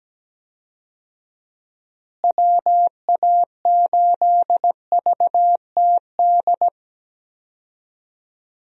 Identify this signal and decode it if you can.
Morse code, heard as a keyed tone, about 17 wpm, 700 Hz. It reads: WA8VTD
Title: USA Call Signs - 6 Characters - Repeated 2x - 17wpm